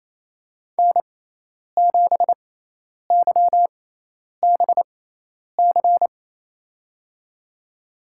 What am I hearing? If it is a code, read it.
Morse code, heard as a keyed tone, about 28 wpm, spaced as for 10 wpm, 700 Hz. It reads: N7YBC
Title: USA Call Signs - 5 Characters - 6x Character Spacing - 28wpm